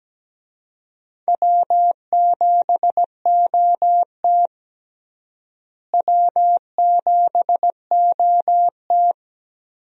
Morse code, heard as a keyed tone, about 17 wpm, 700 Hz. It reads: W7OT W7OT
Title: USA Call Signs - 4 Characters - Repeated 2x - 17wpm